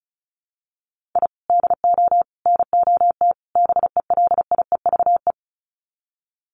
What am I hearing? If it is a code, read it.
Morse code, heard as a keyed tone, about 34 wpm, 700 Hz. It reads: I DO NOT BELIEVE
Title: Sets of 4 Words 35wpm